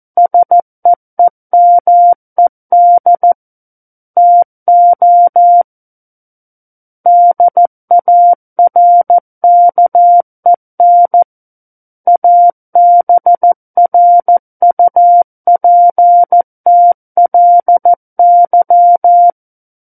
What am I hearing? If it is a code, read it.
Morse code, heard as a keyed tone, about 14 wpm, 700 Hz. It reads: SEEMED TO DARKEN ABRUPTLY
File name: Book One / 05-Chapter 5 - 14 WPM